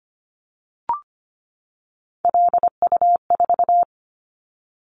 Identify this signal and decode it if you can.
Morse code, heard as a keyed tone, about 25 wpm, 700 Hz. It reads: LU4
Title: Callsign Prefixes for Top 50 DXCC - 25wpm